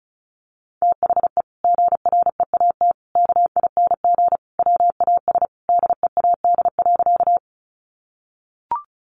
Morse code, heard as a keyed tone, about 34 wpm, 700 Hz. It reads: THE GREAT KING WAS DEAD.
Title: Sentences from Top 1500 Words 35wpm